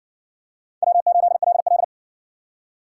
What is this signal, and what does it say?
Morse code, heard as a keyed tone, about 60 wpm, 700 Hz. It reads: W8LL